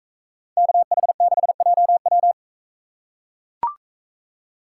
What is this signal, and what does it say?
Morse code, heard as a keyed tone, about 42 wpm, 700 Hz. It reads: KH6JW